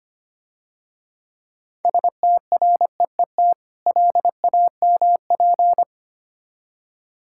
Morse code, heard as a keyed tone, about 25 wpm, 690 Hz. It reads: STREET LAMP
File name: Easy Words - Speed-Racing with Hints - 25wpm